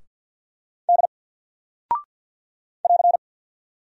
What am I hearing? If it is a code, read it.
Morse code, heard as a keyed tone, about 50 wpm, 700 Hz. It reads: D <AR>
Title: Single Character (Letter:Number:Punctuation:Prosign) 50wpm